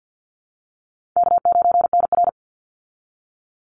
Morse code, heard as a keyed tone, about 50 wpm, 700 Hz. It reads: K9NR
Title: Easy Call Signs Repeated 2x - 50wpm